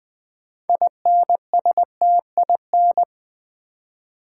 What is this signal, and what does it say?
Morse code, heard as a keyed tone, about 20 wpm, 700 Hz. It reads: INSTIN
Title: Build A Word - 2 Letters At A Time - 8-Letter Words - 20wpm